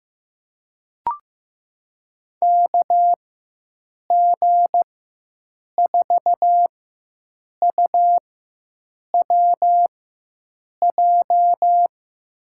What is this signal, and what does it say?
Morse code, heard as a keyed tone, about 16 wpm, spaced as for 7 wpm, 700 Hz. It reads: KG4UWJ
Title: USA Call Signs - 6 Characters - 4x Character Spacing - 15wpm